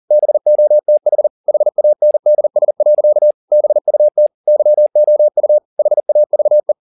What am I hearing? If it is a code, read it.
Morse code, heard as a keyed tone, about 40 wpm, 600 Hz. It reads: BOTH HANDS. BUT YOU HAVE
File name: the old man and the sea 2 40wpm